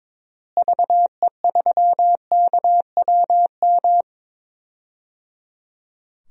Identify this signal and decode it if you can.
Morse code, heard as a keyed tone, about 22 wpm, 700 Hz. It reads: VE3KWM
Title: DXCC-100 Call Signs - 6 Characters - Repeated 3x - 22wpm